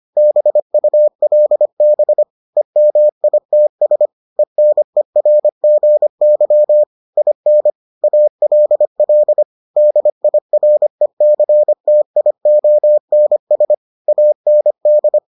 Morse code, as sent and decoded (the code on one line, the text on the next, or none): -... ..- .-.. -... / . -- .. - ... / . -. .-. --. -.-- / .. -. / .- .-.. .-.. / -.. .. .-. -.-. - .. --- -. ... / .- -. -..
BULB EMITS ENERGY IN ALL DIRECTIONS AND